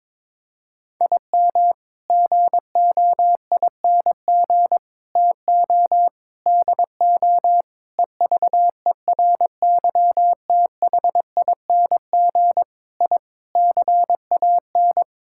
Morse code, heard as a keyed tone, about 21 wpm, 700 Hz. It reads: IM GOING TO DO EVERYTHING I CAN
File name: practice / 22wpm / Sets of 8 Words - 22wpm